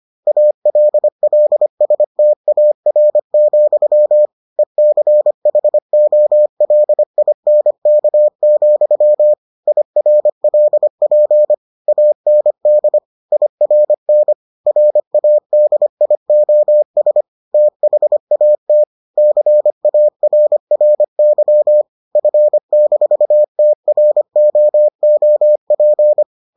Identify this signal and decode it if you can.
Morse code, heard as a keyed tone, about 25 wpm, 600 Hz. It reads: ALLSTAR, ECHOLINK, IRLP AND IRN RADIOS THAT CARRY F-TROOP